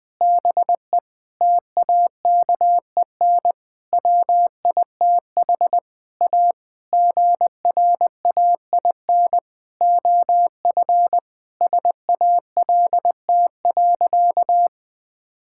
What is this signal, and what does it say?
Morse code, heard as a keyed tone, about 20 wpm, 700 Hz. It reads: BE TAKEN WITH A GRAIN OF SALT.
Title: Sentences with 8-Word Phrases - 20wpm